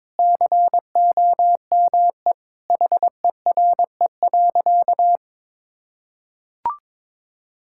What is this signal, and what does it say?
Morse code, heard as a keyed tone, about 22 wpm, 700 Hz. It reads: COME HERE.
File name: practice / 22wpm / Sentences with 2-Word Phrases - 22wpm